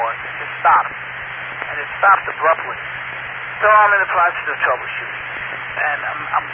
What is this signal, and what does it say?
Emergency Alert System (SAME) tones (header bursts), heard as harsh buzzing bursts, with speech. It is heard from 1.0 to 5.6 s.